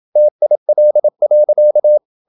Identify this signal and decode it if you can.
Morse code, heard as a keyed tone, about 27 wpm, 600 Hz. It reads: TIL.
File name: Ferie QRQ 27wpm